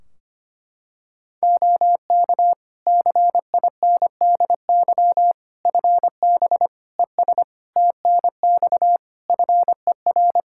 Morse code, heard as a keyed tone, about 25 wpm, 700 Hz. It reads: OK CINDY FB ES TNX FER
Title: Practice QSO - Exchange 2 - Info - 25wpm